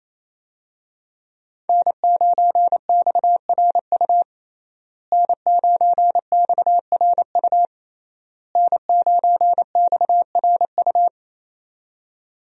Morse code, heard as a keyed tone, about 27 wpm, 700 Hz. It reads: N9XRU N9XRU N9XRU
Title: USA Call Signs - 5 Characters - Repeated 3x - 28wpm